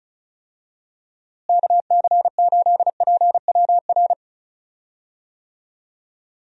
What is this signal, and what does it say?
Morse code, heard as a keyed tone, about 36 wpm, 700 Hz. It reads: KC8PWR
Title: USA Call Signs - 6 Characters - Repeated 3x - 35wpm